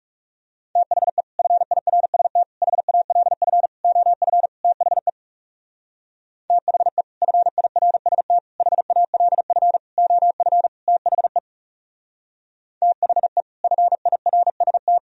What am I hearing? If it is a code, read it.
Morse code, heard as a keyed tone, about 36 wpm, 700 Hz. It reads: THE FIRST HALF OF THE THE FIRST HALF OF THE THE FIRST
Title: Speed-Racing - Sets of 5 Words 30wpm